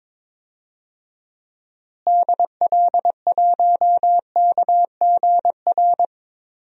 Morse code, heard as a keyed tone, about 22 wpm, 700 Hz. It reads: DL1KGR
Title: DXCC-100 Call Signs - 6 Characters - Repeated 3x - 22wpm